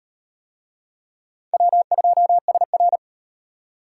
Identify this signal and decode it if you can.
Morse code, heard as a keyed tone, about 37 wpm, 700 Hz. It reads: W2SR